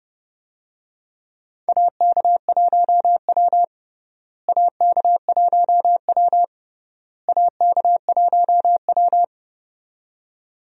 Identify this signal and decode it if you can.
Morse code, heard as a keyed tone, about 31 wpm, 700 Hz. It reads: AK1W AK1W AK1W
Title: USA Call Signs - 4 Characters - Repeated 3x - 30wpm